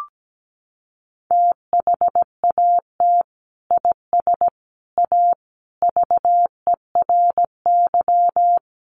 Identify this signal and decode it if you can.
Morse code, heard as a keyed tone, about 17 wpm, 700 Hz. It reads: THAT IS A VERY
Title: Sentences from Top 1200 Words 17wpm